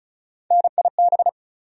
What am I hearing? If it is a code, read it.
Morse code, heard as a keyed tone, about 36 wpm, 700 Hz. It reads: NIB
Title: USA Call Sign Postfixes - 3 Letters 35wpm